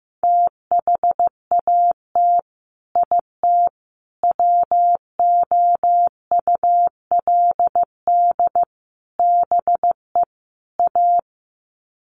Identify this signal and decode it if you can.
Morse code, heard as a keyed tone, about 15 wpm, 700 Hz. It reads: THAT IT WOULD BE A
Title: Sets of 5 Words Encore2 15wpm